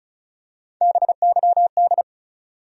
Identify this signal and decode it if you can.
Morse code, heard as a keyed tone, about 36 wpm, 700 Hz. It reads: BYD
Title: USA Call Sign Postfixes - 3 Letters 35wpm